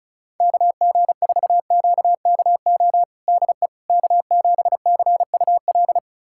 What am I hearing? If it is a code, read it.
Morse code, heard as a keyed tone, about 34 wpm, 700 Hz. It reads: KG4QKO DE K7CUL
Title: USA Callsign DE Callsign 35wpm